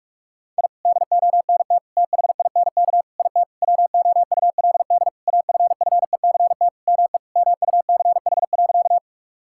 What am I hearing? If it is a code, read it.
Morse code, heard as a keyed tone, about 48 wpm, 700 Hz. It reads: I DONT THINK IT WOULD AFFECT ME MUCH.
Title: Sentences from Top 1000 Words - Review Part 6 - 45wpm